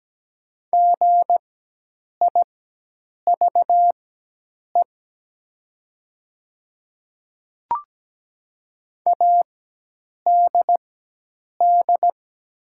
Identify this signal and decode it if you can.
Morse code, heard as a keyed tone, about 17 wpm, spaced as for 8 wpm, 690 Hz. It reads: GIVE ADD
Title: 4x Character Spacing - Top 300 Words - 17wpm